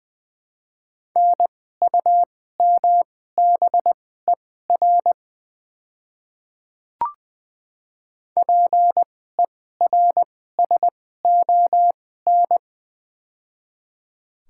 Morse code, heard as a keyed tone, about 20 wpm, spaced as for 15 wpm, 700 Hz. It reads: NUMBER PERSON
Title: 2x Character Spacing - Top 500 Words - 20wpm